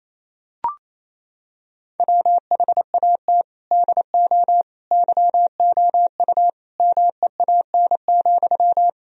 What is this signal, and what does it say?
Morse code, heard as a keyed tone, about 28 wpm, 700 Hz. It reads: WHAT DO YOU MEAN,
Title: Sentences from Top 1000 Words - Review Part 4 - 28wpm